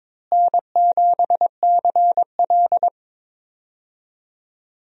Morse code, heard as a keyed tone, about 22 wpm, 700 Hz. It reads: N7CL
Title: USA Call Signs - 4 Characters - Repeated 3x - 22wpm